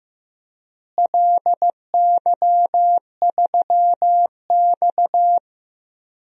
Morse code, heard as a keyed tone, about 15 wpm, 700 Hz. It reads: LY3X
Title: DXCC-100 Call Signs - 4 Characters - Repeated 3x - 15wpm